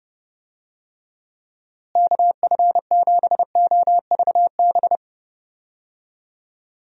Morse code, heard as a keyed tone, about 30 wpm, 700 Hz. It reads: KF7OVB
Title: USA Call Signs - 6 Characters - Repeated 2x - 30wpm